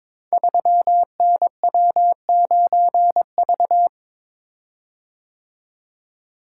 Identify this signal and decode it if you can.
Morse code, heard as a keyed tone, about 22 wpm, 700 Hz. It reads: 3NW9V
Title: Code Groups - 5 Letter:Numbers - 22wpm